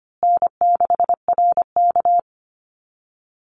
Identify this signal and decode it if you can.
Morse code, heard as a keyed tone, about 25 wpm, 700 Hz. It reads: N6RK
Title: Easy Call Signs Repeated 3x - 25wpm